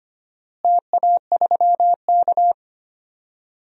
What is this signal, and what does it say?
Morse code, heard as a keyed tone, about 25 wpm, 700 Hz. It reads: TA3K
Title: DXCC-100 Call Signs - 4 Characters - Repeated 3x - 25wpm